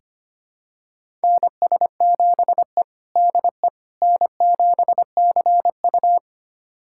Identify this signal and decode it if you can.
Morse code, heard as a keyed tone, about 25 wpm, 700 Hz. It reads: NS7E DE N7CU
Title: USA Callsign DE Callsign - 4 Characters - 25wpm